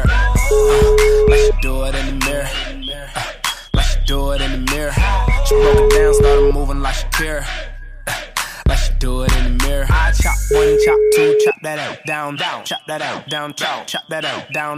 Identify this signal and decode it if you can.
Ringback tone (Europe, 425 Hz), with speech.